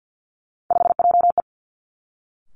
Morse code, heard as a keyed tone, about 50 wpm, 700 Hz. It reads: 5JE
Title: Three Challenging Letter-Number Combinations 50wpm